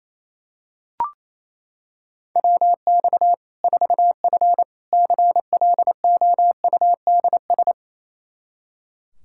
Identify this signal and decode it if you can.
Morse code, heard as a keyed tone, about 28 wpm, 700 Hz. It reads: WX 4F CLOUDS
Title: Practice QSO - Exchange 2 - Info - 28wpm